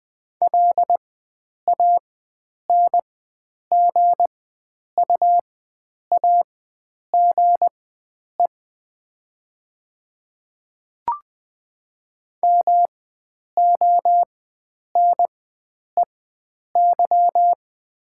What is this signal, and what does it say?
Morse code, heard as a keyed tone, about 20 wpm, spaced as for 9 wpm, 700 Hz. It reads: LANGUAGE MONEY
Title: 4x Character Spacing - Top 500 Words - 20wpm